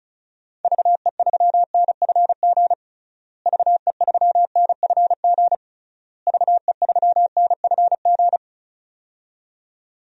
Morse code, heard as a keyed tone, about 34 wpm, 700 Hz. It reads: VE3NFG VE3NFG VE3NFG
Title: DXCC-100 Call Signs - 6 Characters - Repeated 3x - 35wpm